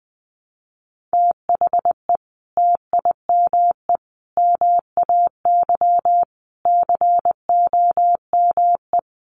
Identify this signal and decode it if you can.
Morse code, heard as a keyed tone, about 20 wpm, 700 Hz. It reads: THE TIME MAY COME
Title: Sentences from Top 1100 Words 20wpm